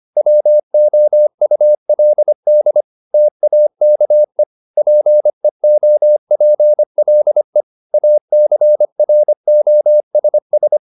Morse code, heard as a keyed tone, about 25 wpm, 600 Hz. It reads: WOULD TAKE PEOPLE ACROSS